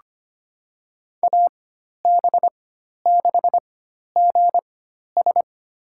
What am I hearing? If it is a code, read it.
Morse code, heard as a keyed tone, about 25 wpm, spaced as for 12 wpm, 700 Hz. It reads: AB6GS
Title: USA Call Signs - 5 Characters - 4x Character Spacing - 25wpm